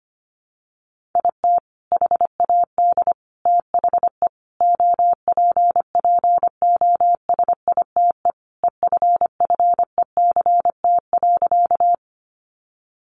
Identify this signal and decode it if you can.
Morse code, heard as a keyed tone, about 25 wpm, 700 Hz. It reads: IT HAD THE OPPOSITE EFFECT.
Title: Sentences from Top 1300 Words 25wpm